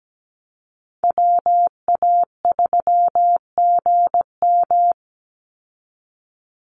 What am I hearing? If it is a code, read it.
Morse code, heard as a keyed tone, about 17 wpm, 700 Hz. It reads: WA3GM
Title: Call Signs - 17wpm